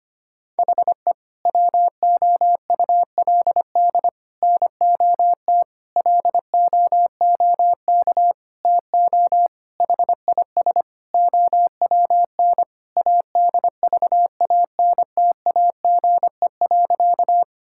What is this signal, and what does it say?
Morse code, heard as a keyed tone, about 25 wpm, 700 Hz. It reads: HE WOULD NOT LOOK TO HIS OWN ADVANTAGE.
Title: Sentences from Top 1000 Words - Review Part 6 - 25wpm